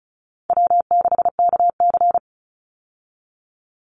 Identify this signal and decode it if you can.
Morse code, heard as a keyed tone, about 36 wpm, 700 Hz. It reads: W6KC
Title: Call Signs - 35wpm